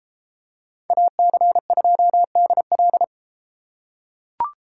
Morse code, heard as a keyed tone, about 33 wpm, 700 Hz. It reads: AC2DL